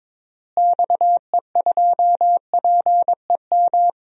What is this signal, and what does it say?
Morse code, heard as a keyed tone, about 22 wpm, 700 Hz. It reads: XE2PEM
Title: DXCC-100 Call Signs - 6 Characters - Repeated 3x - 22wpm